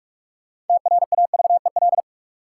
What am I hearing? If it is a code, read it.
Morse code, heard as a keyed tone, about 42 wpm, 700 Hz. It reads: TRAVEL
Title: Sentences from Top 400 Words - New Word Drill - 45wpm